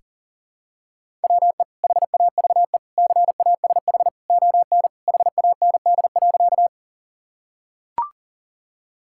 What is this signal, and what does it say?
Morse code, heard as a keyed tone, about 40 wpm, 700 Hz. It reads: WE HAVE CASH ON HAND.
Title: Sentences from Top 1000 Words - Review Part 5 - 40wpm